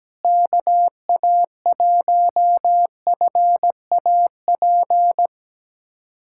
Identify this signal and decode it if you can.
Morse code, heard as a keyed tone, about 17 wpm, 700 Hz. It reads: KA1FAP
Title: USA Callsign DE Callsign - 6 Characters - 17wpm